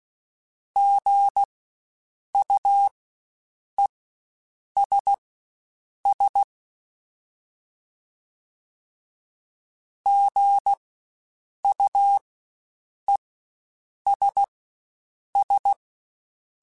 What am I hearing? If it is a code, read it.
Morse code, heard as a keyed tone, about 16 wpm, spaced as for 7 wpm, 800 Hz. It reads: GUESS GUESS
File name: Spaced Alphas / Spaced Alphas mp3s / Gs sp @15